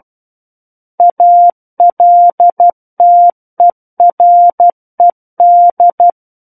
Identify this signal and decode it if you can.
Morse code, heard as a keyed tone, about 12 wpm, 700 Hz. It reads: ALTERED
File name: Book One / 03-Chapter 3 - 12 WPM